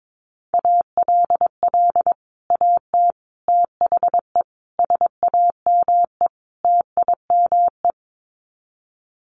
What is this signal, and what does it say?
Morse code, heard as a keyed tone, about 22 wpm, 700 Hz. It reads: ALL AT THE SAME TIME
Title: Sets of 5 Words Encore2 22wpm